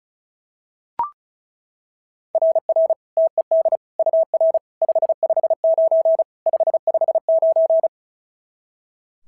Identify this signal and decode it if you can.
Morse code, heard as a keyed tone, about 36 wpm, 690 Hz. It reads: RR TED UR 559 559
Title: SOTA QSO Practice 35wpm